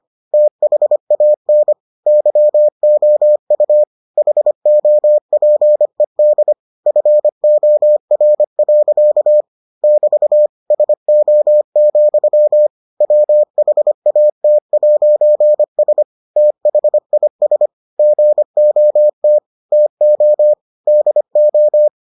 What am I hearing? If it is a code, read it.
Morse code, heard as a keyed tone, about 25 wpm, 600 Hz. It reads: THAN YOU HOPED FOR. <BT> SO, WHAT'S THIS GOT TO DO